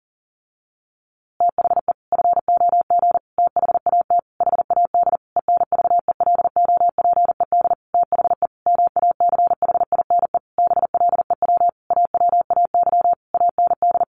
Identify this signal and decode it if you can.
Morse code, heard as a keyed tone, about 40 wpm, 700 Hz. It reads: THE FOG THAT HAD ENVELOPED THE MACHINE BLEW AWAY AND
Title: A Sound Of Thunder 40wpm